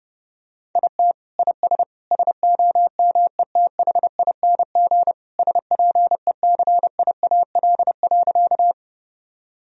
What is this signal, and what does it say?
Morse code, heard as a keyed tone, about 30 wpm, 700 Hz. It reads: IT IS SOMETHING SPECIAL.
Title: Sentences from Top 1000 Words - Review Part 3 - 30wpm